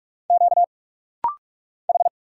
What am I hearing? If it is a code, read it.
Morse code, heard as a keyed tone, about 46 wpm, 700 Hz. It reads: Q H
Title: Single Letter-Number Alt - Rapid-Fire 45wpm